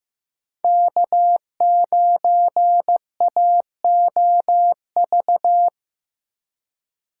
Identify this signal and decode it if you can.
Morse code, heard as a keyed tone, about 15 wpm, 700 Hz. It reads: K9AOV
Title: USA Call Signs - 4x Character Spacing - 15wpm